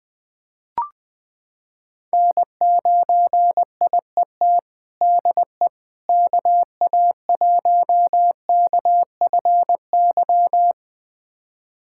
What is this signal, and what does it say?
Morse code, heard as a keyed tone, about 21 wpm, 700 Hz. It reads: N9IET DE KA1KFY
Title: USA Callsign DE Callsign 20wpm